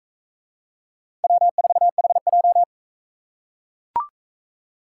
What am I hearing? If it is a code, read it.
Morse code, heard as a keyed tone, about 42 wpm, 700 Hz. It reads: W4HJ